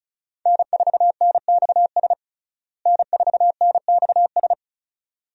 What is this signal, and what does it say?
Morse code, heard as a keyed tone, about 35 wpm, 700 Hz. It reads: N4NXS N4NXS
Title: USA Call Signs - 5 Characters - Repeated 2x - 35wpm